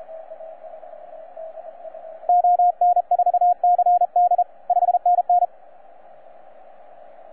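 Morse code, heard as a keyed tone, about 34 wpm, 720 Hz, with radio noise. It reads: ON4CD 5NN
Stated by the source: Z81X on 15 m CW, from DX World.net.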